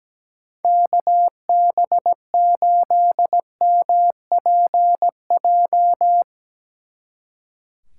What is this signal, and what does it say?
Morse code, heard as a keyed tone, about 19 wpm, 700 Hz. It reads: KB8MPJ
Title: Speed Racing - USA 2x3 Call Signs 17wpm